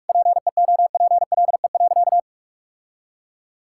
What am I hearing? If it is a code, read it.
Morse code, heard as a keyed tone, about 41 wpm, 700 Hz. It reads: PEOPLE.
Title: Sentences from Top 1000 Words - Review Part 6 - 45wpm